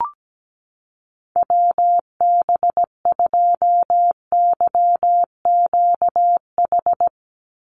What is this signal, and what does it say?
Morse code, heard as a keyed tone, about 17 wpm, 700 Hz. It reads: WB2YQH
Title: Call Signs Repeated 2x - 17wpm